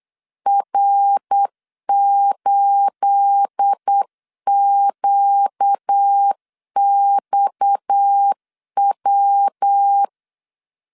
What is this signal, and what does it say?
Morse code, heard as a keyed tone, about 8 wpm, 800 Hz. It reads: R8QXW
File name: LEZIONI / 9 28 8wpm 3512